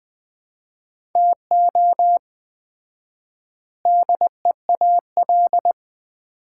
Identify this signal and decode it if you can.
Morse code, heard as a keyed tone, about 20 wpm, 700 Hz. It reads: TO DEAL
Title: 4x Word-Spacing - Sets of 4 Words - 20wpm